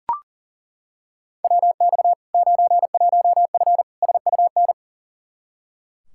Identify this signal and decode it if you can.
Morse code, heard as a keyed tone, about 37 wpm, 700 Hz. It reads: WX 91F SUN
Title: Practice QSO - Exchange 2 - Info - 40wpm